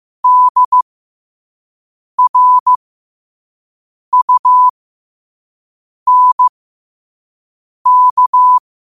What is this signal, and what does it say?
Morse code, heard as a keyed tone, about 15 wpm, spaced as for 5 wpm, 1000 Hz. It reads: DRUNK
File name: qotdfiles / 5wpm